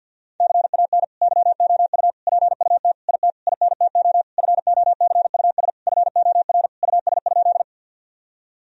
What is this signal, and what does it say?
Morse code, heard as a keyed tone, about 50 wpm, 700 Hz. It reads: CAN YOU PUT IT INTO FOCUS FOR US?
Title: Sentences from Top 1000 Words - Review Part 4 - 50wpm